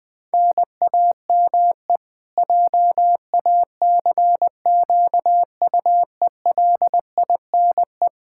Morse code, heard as a keyed tone, about 20 wpm, 700 Hz. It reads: NAME JACQUELINE
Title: Practice QSO - Exchange 1 - Beginning - 20wpm